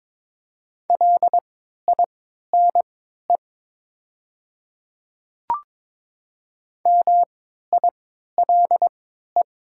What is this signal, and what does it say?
Morse code, heard as a keyed tone, about 22 wpm, spaced as for 13 wpm, 700 Hz. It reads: LINE MILE
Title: 3x Character Spacing - Top 300 Words - 22wpm